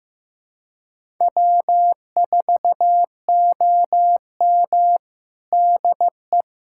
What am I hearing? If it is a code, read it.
Morse code, heard as a keyed tone, about 15 wpm, 700 Hz. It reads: W4OM DE
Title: USA Callsign DE Callsign - 4 Characters - 15wpm